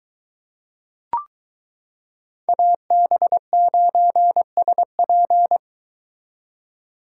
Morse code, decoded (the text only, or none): AB9SP